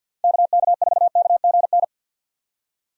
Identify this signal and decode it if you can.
Morse code, heard as a keyed tone, about 48 wpm, 700 Hz. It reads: KK4KGN
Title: USA Call Signs - 6 Characters - 50wpm